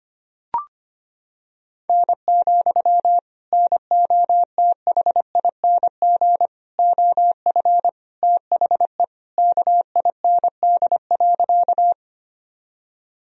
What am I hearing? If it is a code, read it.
Morse code, heard as a keyed tone, about 25 wpm, 700 Hz. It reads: N, NOTHING OF THE KIND.
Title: Sentences from Top 500 Words and Abbreviations 25wpm